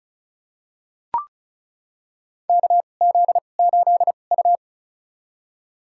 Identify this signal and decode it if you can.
Morse code, heard as a keyed tone, about 36 wpm, spaced as for 25 wpm, 700 Hz. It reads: KZ8U
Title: USA Call Signs - 4 Characters - 2x Character Spacing - 35wpm